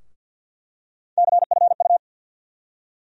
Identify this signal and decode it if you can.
Morse code, heard as a keyed tone, about 47 wpm, 700 Hz. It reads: CFU
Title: USA Call Sign Postfixes - 3 Letters 50wpm